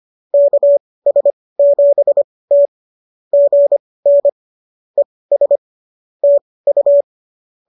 Morse code, heard as a keyed tone, about 25 wpm, spaced as for 18 wpm, 570 Hz. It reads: KS7T GN ES TU